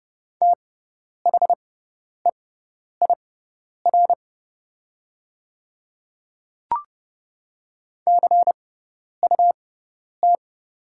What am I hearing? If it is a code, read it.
Morse code, heard as a keyed tone, about 30 wpm, spaced as for 10 wpm, 700 Hz. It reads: THEIR CUT
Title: 6x Character Spacing - Top 300 Words - 30wpm